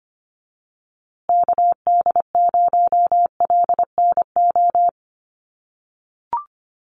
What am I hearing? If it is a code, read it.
Morse code, heard as a keyed tone, about 25 wpm, 700 Hz. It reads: KD0LNO
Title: USA Call Signs Repeated 3x 25wpm